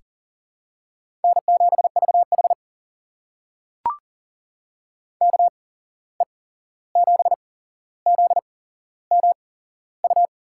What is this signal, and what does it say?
Morse code, heard as a keyed tone, about 41 wpm, spaced as for 15 wpm, 700 Hz. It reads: N7VH KE7ZMU
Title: USA Call Signs - 8x Character Spacing - 40wpm